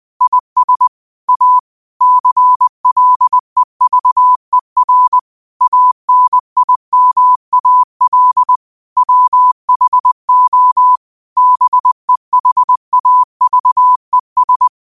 Morse code, heard as a keyed tone, about 20 wpm, 1000 Hz. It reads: IS A CLEVER ANIMAL WHO BEHAVES
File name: qotdfiles / 20wpm